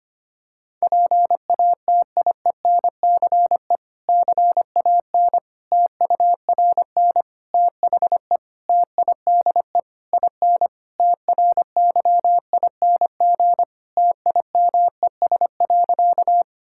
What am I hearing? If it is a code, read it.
Morse code, heard as a keyed tone, about 25 wpm, 700 Hz. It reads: PATIENCE CAN TURN THE TIDE IN TRYING TIMES.
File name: practice / 25wpm / Difficult Sentences with Hints - 25wpm